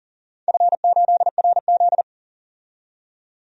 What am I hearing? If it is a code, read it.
Morse code, heard as a keyed tone, about 40 wpm, 700 Hz. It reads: F8RZ
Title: DXCC-100 Call Signs - 4 Characters - 40wpm